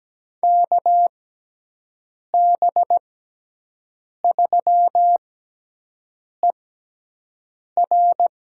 Morse code, heard as a keyed tone, about 17 wpm, spaced as for 6 wpm, 700 Hz. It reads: KB3ER
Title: USA Call Signs - 6 Characters - 6x Character Spacing - 17wpm